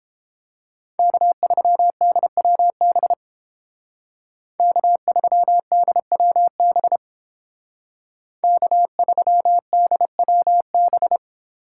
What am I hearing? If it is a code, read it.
Morse code, heard as a keyed tone, about 29 wpm, 700 Hz. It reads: K3DWB K3DWB K3DWB